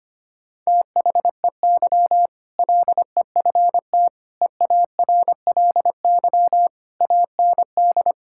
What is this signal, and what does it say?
Morse code, heard as a keyed tone, about 25 wpm, 700 Hz. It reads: THEY LEFT EARLY AND
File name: practice / 25wpm / Sentences with 2-Word Phrases - 25wpm